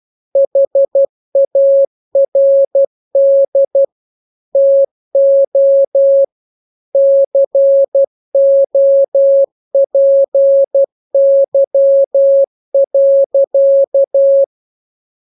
Morse code, heard as a keyed tone, about 12 wpm, 550 Hz. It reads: HARD TO COPY.
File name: jokes / joke 4 12 0000